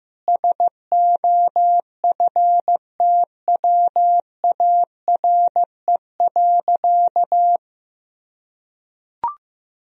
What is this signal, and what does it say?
Morse code, heard as a keyed tone, about 15 wpm, 700 Hz. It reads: SOFTWARE.
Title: Sentences from Top 1000 Words - Review Part 3 - 15wpm